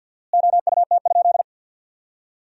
Morse code, heard as a keyed tone, about 52 wpm, 700 Hz. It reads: OUT?
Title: 4x Word-Spacing - Sentences from Top 300 Words - 50wpm